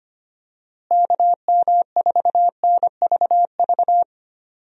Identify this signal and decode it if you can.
Morse code, heard as a keyed tone, about 25 wpm, 700 Hz. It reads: KM4NVV
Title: USA Call Signs - 6 Characters - Repeated 2x - 25wpm